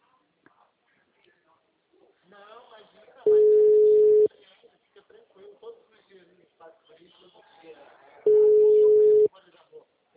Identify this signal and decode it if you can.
Ringback tone (Europe, 425 Hz), with speech.